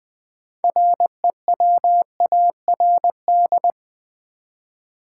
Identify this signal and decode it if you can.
Morse code, heard as a keyed tone, about 20 wpm, 700 Hz. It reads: REWARD
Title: Six Letter Words - 20wpm